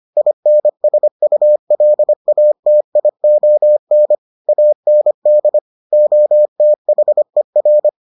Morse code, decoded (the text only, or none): INSULATION AND OTHER